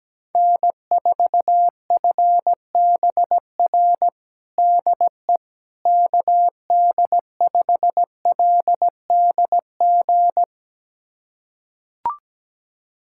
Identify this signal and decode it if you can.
Morse code, heard as a keyed tone, about 17 wpm, 700 Hz. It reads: N4FBR DE KD5LDG
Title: USA Callsign DE Callsign 17wpm